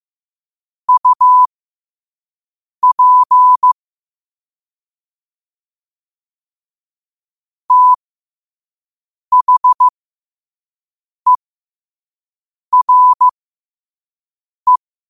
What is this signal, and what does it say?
Morse code, heard as a keyed tone, about 15 wpm, spaced as for 5 wpm, 1000 Hz. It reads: UP THERE
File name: qotdfiles / 5wpm